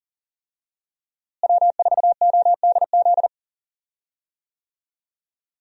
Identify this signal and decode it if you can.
Morse code, heard as a keyed tone, about 40 wpm, 700 Hz. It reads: W4ODZ
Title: USA Call Signs - 5 Characters - Repeated 3x - 40wpm